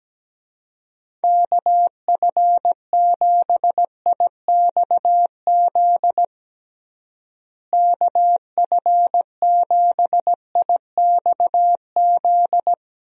Morse code, heard as a keyed tone, about 17 wpm, 700 Hz. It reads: KF7IXZ KF7IXZ
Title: USA Call Signs - 6 Characters - Repeated 2x - 17wpm